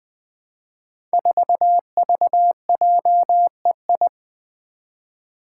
Morse code, heard as a keyed tone, about 20 wpm, 700 Hz. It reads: 4VJEI
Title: Five Challenging Letter-Number Combinations 20wpm